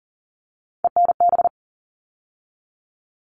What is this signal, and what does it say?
Morse code, heard as a keyed tone, about 40 wpm, 700 Hz. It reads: ENB
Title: All Cut Numbers - 3-Digit 40wpm